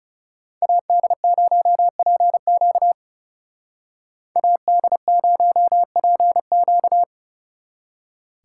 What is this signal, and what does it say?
Morse code, heard as a keyed tone, about 31 wpm, 700 Hz. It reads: AD0PQ AD0PQ
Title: Speed Racing - USA Call Signs 30wpm